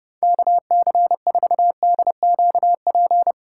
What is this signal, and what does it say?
Morse code, heard as a keyed tone, about 32 wpm, 700 Hz. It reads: KC4DQP
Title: USA Callsign DE Callsign 30wpm